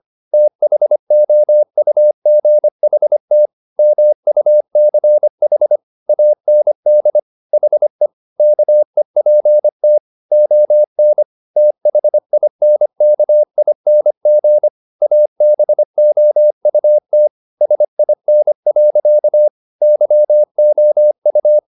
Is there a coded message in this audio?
Yes: Morse code from 0.3 to 21.6 s, heard as a keyed tone, about 25 wpm, 600 Hz. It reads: THOUGHT MUCH AND HE KEPT ON THINKING ABOUT SIN. YOU